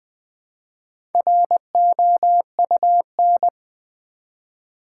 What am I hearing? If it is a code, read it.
Morse code, heard as a keyed tone, about 20 wpm, 700 Hz. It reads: ROUN
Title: Build A Word - 1 Letter At A Time - 3 to 5-Letter Words - 20wpm